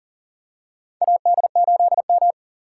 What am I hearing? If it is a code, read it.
Morse code, heard as a keyed tone, about 40 wpm, 700 Hz. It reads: AD8M
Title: USA Call Signs - 4 Characters - 3x Character Spacing - 40wpm